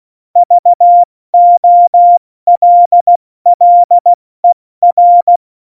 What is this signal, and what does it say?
Morse code, heard as a keyed tone, about 16 wpm, 700 Hz. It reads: VOLLER
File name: Morsekurs / Witze / Witz 17. November 2020, 70 F80